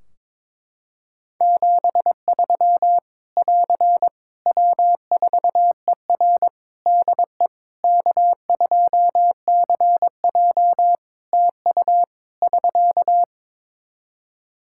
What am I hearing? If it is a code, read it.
Morse code, heard as a keyed tone, about 22 wpm, 700 Hz. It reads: 73 <AR> W4ER DE K2CJ TU <SK>
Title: Practice QSO - Exchange 4 - Ending - 22wpm